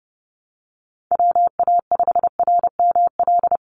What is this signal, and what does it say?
Morse code, heard as a keyed tone, about 30 wpm, 700 Hz. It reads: WA5RML
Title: Call Signs - 30wpm